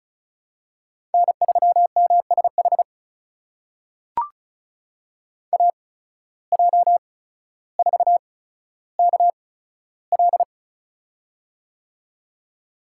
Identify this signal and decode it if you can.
Morse code, heard as a keyed tone, about 34 wpm, spaced as for 15 wpm, 700 Hz. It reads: N3MSH AJ4KL
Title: USA Call Signs - 5 Characters - 8x Character Spacing - 35wpm